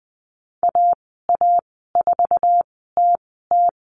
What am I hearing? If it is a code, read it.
Morse code, heard as a keyed tone, about 20 wpm, spaced as for 14 wpm, 700 Hz. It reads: AA4TT
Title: 062 Lesson 11g - Letter h - Call Signs 20wpm